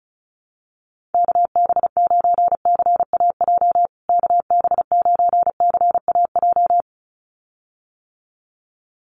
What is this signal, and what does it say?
Morse code, heard as a keyed tone, about 34 wpm, 700 Hz. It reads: KB9CAJ KB9CAJ
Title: USA Call Signs Repeated 2x 35wpm